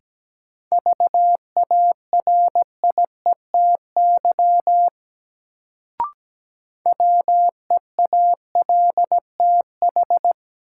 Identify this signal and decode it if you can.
Morse code, heard as a keyed tone, about 17 wpm, 700 Hz. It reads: VARIETY WEALTH
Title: Sentences from Top 1000 Words - Review Part 4 - New Word Drill - 17wpm